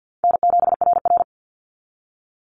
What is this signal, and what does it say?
Morse code, heard as a keyed tone, about 50 wpm, 700 Hz. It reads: N7RR
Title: NA SOTA Activator Callsigns 50wpm